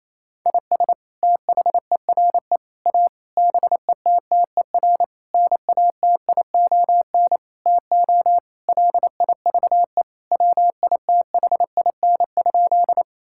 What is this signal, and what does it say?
Morse code, heard as a keyed tone, about 28 wpm, 700 Hz. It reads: IS THERE A BETTER NATION TO LIVE WITHIN?
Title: Sentences from Top 1000 Words - Review Part 6 - 28wpm